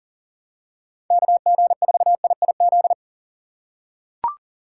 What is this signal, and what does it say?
Morse code, heard as a keyed tone, about 37 wpm, 700 Hz. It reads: KG4IIZ
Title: USA Call Signs - 6 Characters - 3x Character Spacing - 40wpm